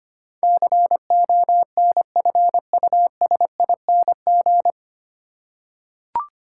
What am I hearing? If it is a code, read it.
Morse code, heard as a keyed tone, about 25 wpm, 700 Hz. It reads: CONFUSING
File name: practice / 25wpm / 9 Letter Words - First 3 Letters Given - 25wpm